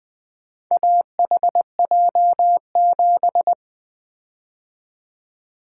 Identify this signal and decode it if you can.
Morse code, heard as a keyed tone, about 20 wpm, 700 Hz. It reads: AHJ7
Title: Four Letter-Number Combinations - 20wpm